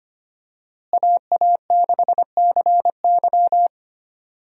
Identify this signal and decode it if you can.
Morse code, heard as a keyed tone, about 25 wpm, 700 Hz. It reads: AA6CY
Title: USA Call Signs - 5 Characters - Repeated 3x - 25wpm